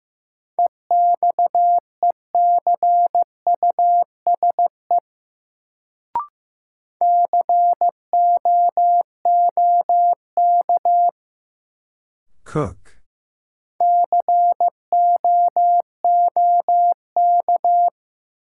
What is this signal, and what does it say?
Morse code, heard as a keyed tone, about 15 wpm, 700 Hz. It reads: EXECUSE COOK COOK
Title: Sentences from Top 700 Words - New Word Drill - 15wpm